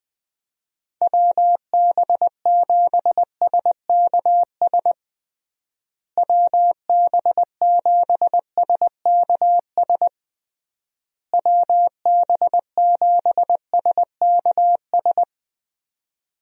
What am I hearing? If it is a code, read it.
Morse code, heard as a keyed tone, about 21 wpm, 700 Hz. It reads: WB7SKS WB7SKS WB7SKS
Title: USA Call Signs - 6 Characters - Repeated 3x - 20wpm